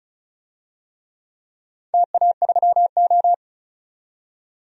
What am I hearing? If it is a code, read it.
Morse code, heard as a keyed tone, about 36 wpm, 690 Hz. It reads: TA3O
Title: DXCC-100 Call Signs - 4 Characters - 35wpm